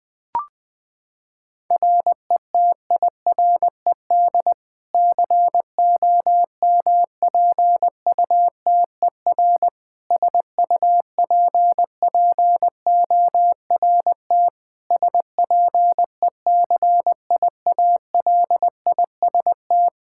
Morse code, heard as a keyed tone, about 20 wpm, 700 Hz. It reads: RETIRED COMPUTER SUPPORT SPECIALIST
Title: Practice QSO - Exchange 3 - More Info - 20wpm